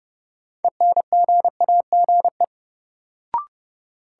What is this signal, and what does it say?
Morse code, heard as a keyed tone, about 30 wpm, 700 Hz. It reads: ENGAGE
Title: Sentences from Top 1500 Words - New Word Drill Alt - 30wpm